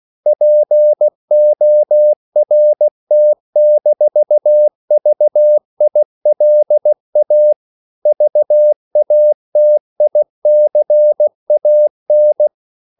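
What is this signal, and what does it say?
Morse code, heard as a keyed tone, about 16 wpm, 600 Hz. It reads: PORT-VILA VATICAN
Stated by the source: CW practice of national capitals at 16 words per minute, from ON6ZQ.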